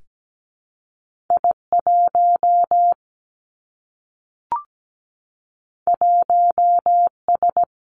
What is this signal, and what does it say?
Morse code, heard as a keyed tone, about 17 wpm, 700 Hz. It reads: I1 1S
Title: Two Challenging Letter-Number Combinations 17wpm